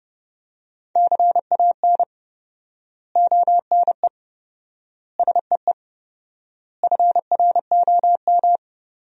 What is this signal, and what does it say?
Morse code, heard as a keyed tone, about 32 wpm, 700 Hz. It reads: CAN ONE SEE FROM
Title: 4x Word-Spacing - Sentences from Top 100 Words - 30wpm